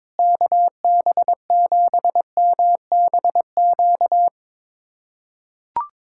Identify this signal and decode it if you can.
Morse code, heard as a keyed tone, about 22 wpm, 700 Hz. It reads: KB7MBQ
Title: USA Callsign DE Callsign - 6 Characters - 22wpm